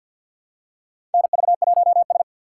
Morse code, heard as a keyed tone, about 51 wpm, 700 Hz. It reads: NV1S
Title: USA Call Signs - 4 Characters - 2x Character Spacing - 50wpm